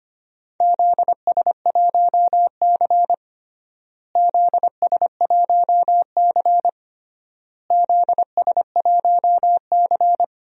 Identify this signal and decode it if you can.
Morse code, heard as a keyed tone, about 25 wpm, 700 Hz. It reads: ZS1C ZS1C ZS1C
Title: DXCC-100 Call Signs - 4 Characters - Repeated 3x - 25wpm